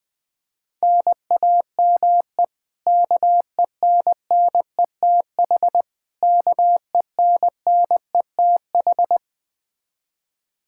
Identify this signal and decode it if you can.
Morse code, heard as a keyed tone, about 19 wpm, 700 Hz. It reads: NAME KENNETH KENNETH
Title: Practice QSO - Exchange 1 - Beginning - 20wpm